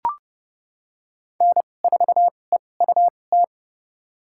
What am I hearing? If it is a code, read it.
Morse code, heard as a keyed tone, about 29 wpm, spaced as for 21 wpm, 700 Hz. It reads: N4EUT
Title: USA Call Signs - 5 Characters - 2x Character Spacing - 30wpm